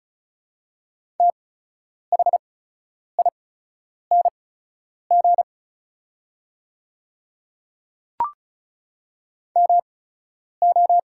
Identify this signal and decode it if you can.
Morse code, heard as a keyed tone, about 35 wpm, spaced as for 10 wpm, 700 Hz. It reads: THING MO
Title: 8x Character Spacing - Top 100 Words - 35wpm